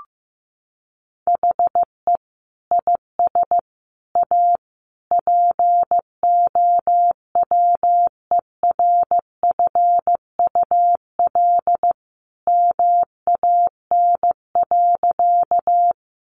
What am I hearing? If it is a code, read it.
Morse code, heard as a keyed tone, about 15 wpm, 700 Hz. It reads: HE IS A POWERFUL MAN.
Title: Sentences from Top 1300 Words 15wpm